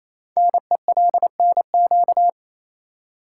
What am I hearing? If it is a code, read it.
Morse code, heard as a keyed tone, about 27 wpm, 700 Hz. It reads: NELNQ
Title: Code Groups - 5 Letters - 28wpm